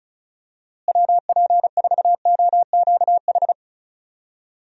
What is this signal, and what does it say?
Morse code, heard as a keyed tone, about 36 wpm, 700 Hz. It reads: WP4OQH
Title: USA Call Signs - 6 Characters - Repeated 2x - 35wpm